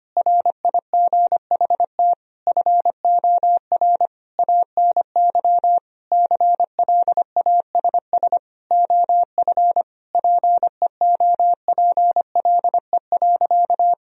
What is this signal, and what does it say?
Morse code, heard as a keyed tone, about 25 wpm, 700 Hz. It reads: RIGHT FOR ANY CLASS OF PEOPLE.
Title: Sentences from Top 1000 Words - Review Part 2 - 25wpm